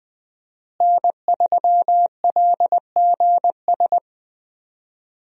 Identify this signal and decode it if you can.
Morse code, heard as a keyed tone, about 20 wpm, 700 Hz. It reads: N3LGS
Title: USA Call Signs - 5 Characters - Repeated 3x - 20wpm